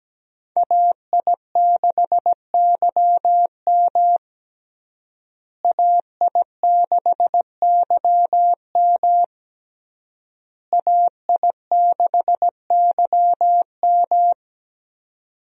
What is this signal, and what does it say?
Morse code, heard as a keyed tone, about 17 wpm, 700 Hz. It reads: AI6YM AI6YM AI6YM
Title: USA Call Signs - 5 Characters - Repeated 3x - 17wpm